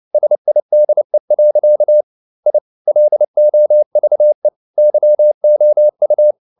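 Morse code, heard as a keyed tone, about 29 wpm, 600 Hz. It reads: SIDE. I LOVE YOU
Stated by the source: Morse code at 29 wpm, sending Ken Follett's novel Never in English, from CW med Gnister.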